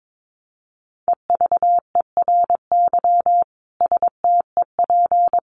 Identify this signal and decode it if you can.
Morse code, heard as a keyed tone, about 22 wpm, 700 Hz. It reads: EVERY STEP
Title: Sets of 5 Words Encore2 22wpm